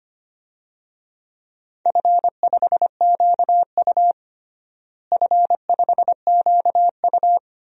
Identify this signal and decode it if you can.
Morse code, heard as a keyed tone, about 25 wpm, 700 Hz. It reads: F5QU F5QU
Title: DXCC-100 Call Signs - 4 Characters - Repeated 3x - 25wpm